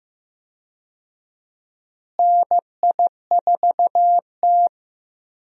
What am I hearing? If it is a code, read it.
Morse code, heard as a keyed tone, about 15 wpm, 700 Hz. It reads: NI4T
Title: USA Call Signs - 4 Characters - Repeated 2x - 15wpm